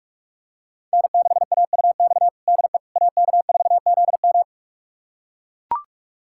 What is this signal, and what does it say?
Morse code, heard as a keyed tone, about 44 wpm, 700 Hz. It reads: N6AUX DE AK4ZM
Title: USA Callsign DE Callsign - 5 Characters - 45wpm